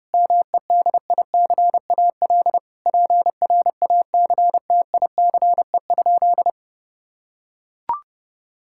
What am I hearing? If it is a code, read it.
Morse code, heard as a keyed tone, about 30 wpm, 700 Hz. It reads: MEDICAL PRACTICE?
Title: Sentences from Top 1000 Words - Review Part 5 - 30wpm